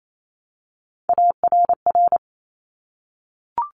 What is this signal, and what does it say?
Morse code, heard as a keyed tone, about 28 wpm, 700 Hz. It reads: ARR
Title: Top Three Letter Combinations in Words 28wpm